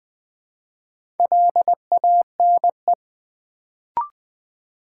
Morse code, heard as a keyed tone, about 20 wpm, 700 Hz. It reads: LANE
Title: Build A Word - 2 Letters At A Time - 4-Letter Words - 20wpm